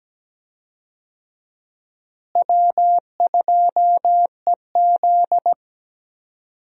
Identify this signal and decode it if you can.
Morse code, heard as a keyed tone, about 17 wpm, 700 Hz. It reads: W2EZ
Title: USA Call Signs - 4 Characters - Repeated 3x - 17wpm